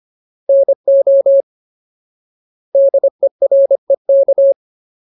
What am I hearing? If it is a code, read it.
Morse code, heard as a keyed tone, about 25 wpm, 550 Hz. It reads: NO DEREK
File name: Ultimate Word List / 25WPM Ultimate